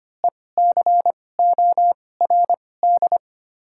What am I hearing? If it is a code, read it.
Morse code, heard as a keyed tone, about 25 wpm, spaced as for 18 wpm, 700 Hz. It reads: ECORD
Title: 2x Character Spacing - Top 500 Words - 25wpm